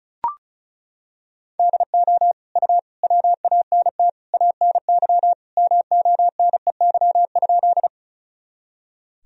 Morse code, heard as a keyed tone, about 34 wpm, 700 Hz. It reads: DO U WANT ANY MONEY?
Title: Sentences from Top 400 Words and Abbreviations 35wpm